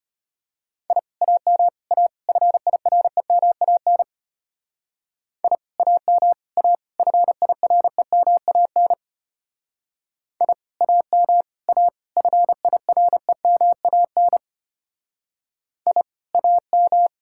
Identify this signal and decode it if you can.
Morse code, heard as a keyed tone, about 29 wpm, 700 Hz. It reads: I AM A FIREMAN I AM A FIREMAN I AM A FIREMAN I AM
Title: Speed-Racing Common QSO Phrases Encore - 25wpm